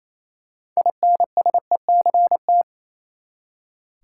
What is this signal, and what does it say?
Morse code, heard as a keyed tone, about 28 wpm, 700 Hz. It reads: INSECT
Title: Six Letter Words - 28wpm